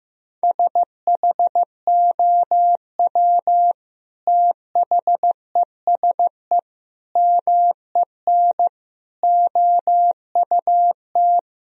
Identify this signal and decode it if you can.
Morse code, heard as a keyed tone, about 14 wpm, 700 Hz. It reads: SHOW THESE MEN OUT
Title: Sentences from Top 1000 Words - Review Part 1 - 15wpm